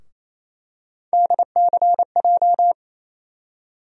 Morse code, heard as a keyed tone, about 27 wpm, 700 Hz. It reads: DCJ
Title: USA Call Sign Postfixes - 3 Letters 28wpm